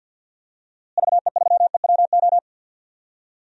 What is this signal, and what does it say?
Morse code, heard as a keyed tone, about 51 wpm, 700 Hz. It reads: VE3EWO
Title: DXCC-100 Call Signs - 6 Characters - Repeated 3x - 50wpm